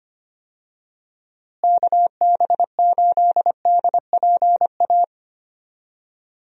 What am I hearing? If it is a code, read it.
Morse code, heard as a keyed tone, about 25 wpm, 700 Hz. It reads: KB8DPA
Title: USA Call Signs - 6 Characters - Repeated 2x - 25wpm